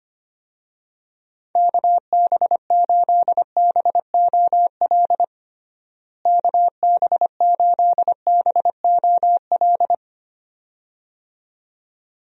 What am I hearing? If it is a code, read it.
Morse code, heard as a keyed tone, about 25 wpm, 700 Hz. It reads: KB8BOL KB8BOL
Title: USA Call Signs - 6 Characters - Repeated 2x - 25wpm